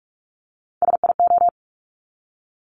Morse code, heard as a keyed tone, about 46 wpm, 700 Hz. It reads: SIO
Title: Top Three Letter Combinations in Words 45wpm